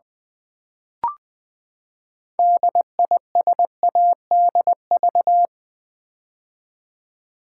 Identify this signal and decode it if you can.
Morse code, heard as a keyed tone, about 20 wpm, 700 Hz. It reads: DISADV
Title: Build A Word - 6 Letters At A Time - 12-Letter Words - 20wpm